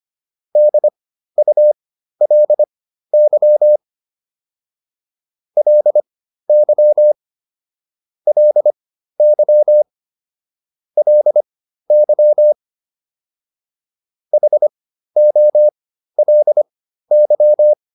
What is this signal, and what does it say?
Morse code, heard as a keyed tone, about 25 wpm, spaced as for 13 wpm, 600 Hz. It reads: DULY LY LY LY HOLY